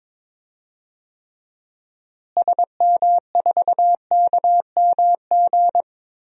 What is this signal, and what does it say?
Morse code, heard as a keyed tone, about 22 wpm, 700 Hz. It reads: SM4KMG
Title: DXCC-100 Call Signs - 6 Characters - Repeated 3x - 22wpm